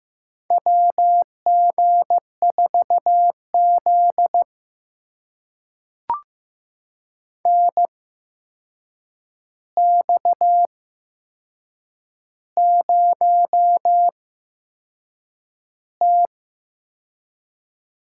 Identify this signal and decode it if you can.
Morse code, heard as a keyed tone, about 15 wpm, spaced as for 6 wpm, 700 Hz. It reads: WG4Z NX0T
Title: USA Call Signs - 4 Characters - 8x Character Spacing - 15wpm